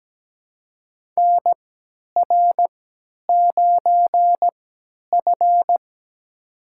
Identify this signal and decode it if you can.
Morse code, heard as a keyed tone, about 17 wpm, spaced as for 10 wpm, 700 Hz. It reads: NR9F
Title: USA Call Signs - 4 Characters - 3x Character Spacing - 17wpm